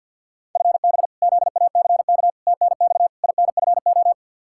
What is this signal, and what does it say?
Morse code, heard as a keyed tone, about 53 wpm, 700 Hz. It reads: FB ZACK TNX INFO
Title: Practice QSO - Exchange 3 - More Info - 50wpm